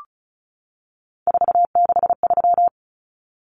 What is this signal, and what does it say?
Morse code, heard as a keyed tone, about 34 wpm, 700 Hz. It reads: 463
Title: Common Cut Numbers - 3-Digit 35wpm